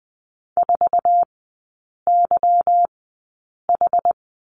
Morse code, heard as a keyed tone, about 20 wpm, spaced as for 8 wpm, 700 Hz. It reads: 4YH
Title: 172 Lesson 28e - Number 9 - Three Characters 20wpm